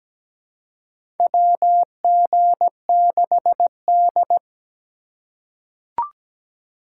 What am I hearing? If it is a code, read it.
Morse code, heard as a keyed tone, about 17 wpm, 700 Hz. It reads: WG6D
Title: USA Call Signs - 4 Characters - 4x Character Spacing - 17wpm